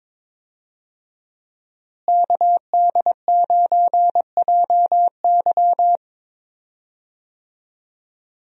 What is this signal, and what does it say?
Morse code, heard as a keyed tone, about 22 wpm, 700 Hz. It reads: KD9JY
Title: USA Call Signs - 5 Characters - Repeated 2x - 22wpm